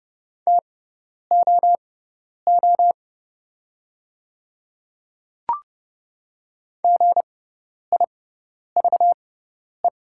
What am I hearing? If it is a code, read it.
Morse code, heard as a keyed tone, about 30 wpm, spaced as for 10 wpm, 690 Hz. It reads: TOO GIVE
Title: 6x Character Spacing - Top 200 Words - 30wpm